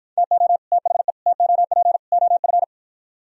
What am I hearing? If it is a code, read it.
Morse code, heard as a keyed tone, about 52 wpm, 700 Hz. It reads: TO THE TOP OF